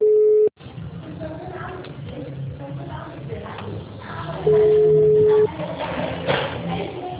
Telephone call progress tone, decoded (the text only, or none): ringback tone